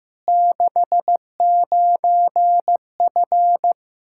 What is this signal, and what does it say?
Morse code, heard as a keyed tone, about 15 wpm, 700 Hz. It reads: 69F
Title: K1USN SST QSO Practice 15wpm